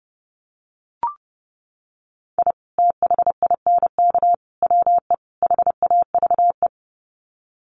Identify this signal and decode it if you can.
Morse code, heard as a keyed tone, about 30 wpm, 700 Hz. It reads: I THINK WE HAVE
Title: Sets of 4 Words 30wpm